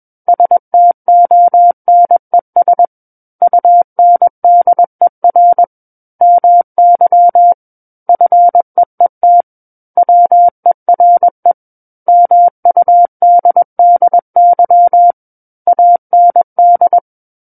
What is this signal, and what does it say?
Morse code, heard as a keyed tone, about 21 wpm, 700 Hz. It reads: STONES UNDER MY FEET WERE MUDDY AND
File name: Book One / 12-Chapter 12 - 21 WPM